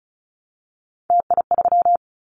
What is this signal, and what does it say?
Morse code, heard as a keyed tone, about 36 wpm, 700 Hz. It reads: TI3
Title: Callsign Prefixes for Top 100 DXCC - 35wpm